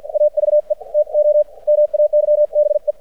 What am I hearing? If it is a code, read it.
Morse code, heard as a keyed tone, about 53 wpm, spaced as for 45 wpm, 600 Hz. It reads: AVE TO MAYBE
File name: archived / CRSnet